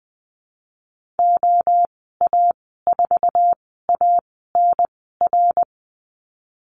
Morse code, heard as a keyed tone, about 19 wpm, spaced as for 14 wpm, 700 Hz. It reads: OA4ANR
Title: 088 Lesson 15g - Number 5 - Call Signs 20wpm